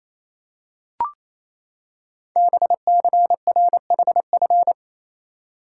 Morse code, heard as a keyed tone, about 28 wpm, 700 Hz. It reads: BCRHF
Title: Code Groups - 5 Letters - 28wpm